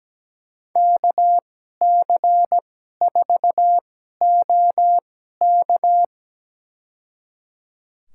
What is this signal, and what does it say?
Morse code, heard as a keyed tone, about 17 wpm, spaced as for 12 wpm, 700 Hz. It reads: KC4OK
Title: USA Call Signs - 5 Characters - 2x Character Spacing - 17wpm